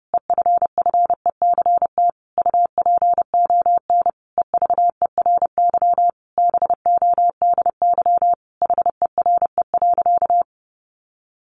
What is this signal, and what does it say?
Morse code, heard as a keyed tone, about 30 wpm, 700 Hz. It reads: EFFECT UPON EVERY BODY HERE.
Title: Sentences from Top 1100 Words 30wpm